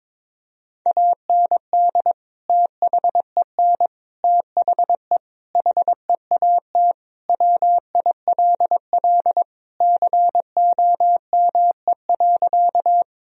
Morse code, heard as a keyed tone, about 22 wpm, 700 Hz. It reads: AND THEN THE HEAT WILL COME.
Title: Sentences from Top 1000 Words - Review Part 3 - 22wpm